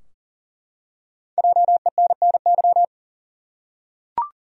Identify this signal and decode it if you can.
Morse code, heard as a keyed tone, about 40 wpm, 700 Hz. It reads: JENNY
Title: USA Common Names - Max 5 Letters - 40wpm